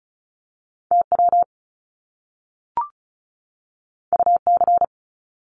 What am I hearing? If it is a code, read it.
Morse code, heard as a keyed tone, about 36 wpm, 700 Hz. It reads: TW UC
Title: Top Two Letter Combinations in Words 35wpm